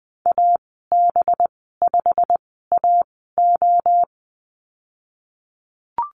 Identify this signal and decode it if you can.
Morse code, heard as a keyed tone, about 20 wpm, spaced as for 15 wpm, 700 Hz. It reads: AB5AO
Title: 193 Lesson 31g - Letter v - Call Signs 20wpm